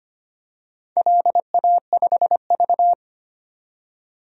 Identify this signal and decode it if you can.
Morse code, heard as a keyed tone, about 25 wpm, 700 Hz. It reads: LA5V
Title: DXCC-100 Call Signs - 4 Characters - 25wpm